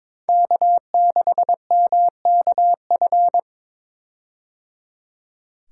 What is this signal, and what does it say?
Morse code, heard as a keyed tone, about 22 wpm, 700 Hz. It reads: K6MKF
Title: K1USN SST QSO Practice - 22wpm